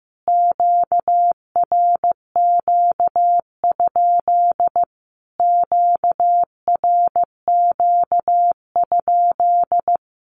Morse code, heard as a keyed tone, about 15 wpm, 700 Hz. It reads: QRQ? QRQ?
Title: Q-Codes Repeated 3x 15wpm